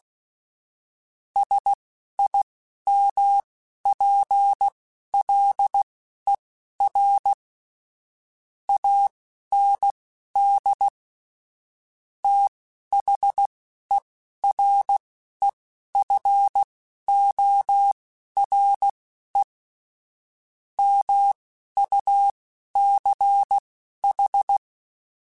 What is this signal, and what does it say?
Morse code, heard as a keyed tone, about 16 wpm, spaced as for 11 wpm, 800 Hz. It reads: SIMPLER AND THEREFORE MUCH
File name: Technical Subjects / Technical Subjects-mp3s / AND Gate @10